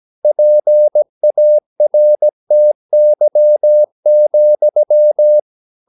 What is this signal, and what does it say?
Morse code, heard as a keyed tone, about 17 wpm, 600 Hz. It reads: PARTY,
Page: Romanen Never af Ken Follet på engelsk.